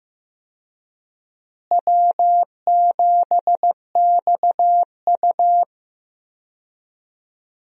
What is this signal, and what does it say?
Morse code, heard as a keyed tone, about 15 wpm, 700 Hz. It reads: W7XU
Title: USA Call Signs - 4 Characters - Repeated 2x - 15wpm